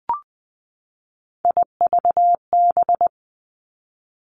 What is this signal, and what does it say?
Morse code, heard as a keyed tone, about 20 wpm, 700 Hz. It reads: IVB
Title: Three Challenging Letter-Number Combinations 20wpm